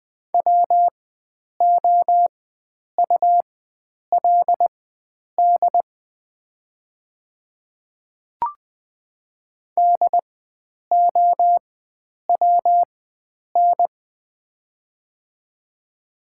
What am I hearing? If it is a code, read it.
Morse code, heard as a keyed tone, about 20 wpm, spaced as for 9 wpm, 700 Hz. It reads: WOULD DOWN
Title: 4x Character Spacing - Top 300 Words - 20wpm